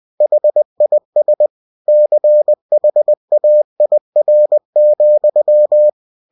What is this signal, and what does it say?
Morse code, heard as a keyed tone, about 20 wpm, 600 Hz. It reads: HIS CHAIR,
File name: cw-books / winnie-the-pooh / winnie-20wpm-page-0093